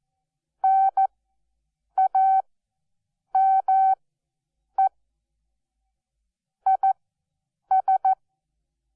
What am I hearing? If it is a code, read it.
Morse code, heard as a keyed tone, about 14 wpm, spaced as for 7 wpm, 770 Hz. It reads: NAME IS
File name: Onlinelehrgang / morsen / DJ4UF Morsen12